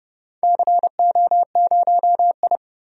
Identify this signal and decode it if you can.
Morse code, heard as a keyed tone, about 30 wpm, 700 Hz. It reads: CO0I
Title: DXCC-100 Call Signs - 4 Characters - 30wpm